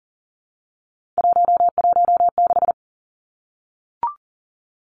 Morse code, heard as a keyed tone, about 40 wpm, 700 Hz. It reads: 116
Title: Numbers - 3-Digit 40wpm